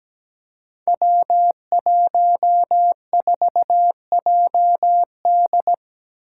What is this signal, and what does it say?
Morse code, heard as a keyed tone, about 17 wpm, 700 Hz. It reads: W14JD
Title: Five Challenging Letter-Number Combinations 17wpm